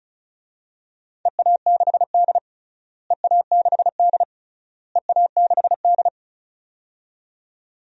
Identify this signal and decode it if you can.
Morse code, heard as a keyed tone, about 36 wpm, 700 Hz. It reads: EA6D EA6D EA6D
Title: DXCC-100 Call Signs - 4 Characters - Repeated 3x - 35wpm